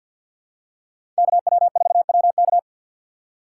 Morse code, heard as a keyed tone, about 53 wpm, 700 Hz. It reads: KW4WK
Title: USA Call Signs - 5 Characters - 2x Character Spacing - 50wpm